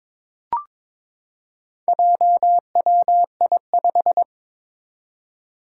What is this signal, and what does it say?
Morse code, heard as a keyed tone, about 22 wpm, 700 Hz. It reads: JWI5
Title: Four Challenging Letter-Number Combinations 22wpm